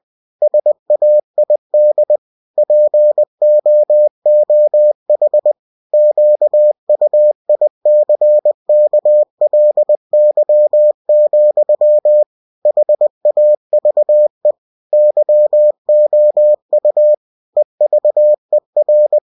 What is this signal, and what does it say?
Morse code, heard as a keyed tone, about 20 wpm, 600 Hz. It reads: SAID POOH QUICKLY, HAVE YOU EVER
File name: cw-books / winnie-the-pooh / winnie-20wpm-page-0068